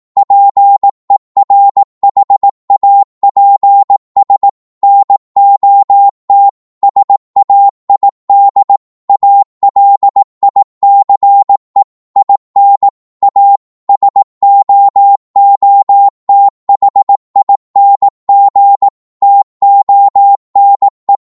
Morse code, heard as a keyed tone, about 18 wpm, 800 Hz. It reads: PERHAPS NOT SAID ALICE IN A SOOTHING TONE
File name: Morse / psg-18wpm